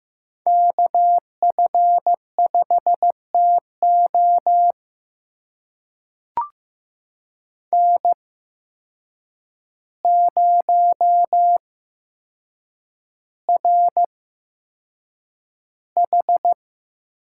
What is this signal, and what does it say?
Morse code, heard as a keyed tone, about 15 wpm, spaced as for 7 wpm, 700 Hz. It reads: KF5TO N0RH
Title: USA Call Signs - 5 Characters - 8x Character Spacing - 15wpm